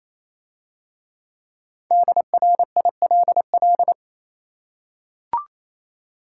Morse code, heard as a keyed tone, about 27 wpm, 690 Hz. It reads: DRILL
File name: Easy Words - Speed-Racing with Hints - 22wpm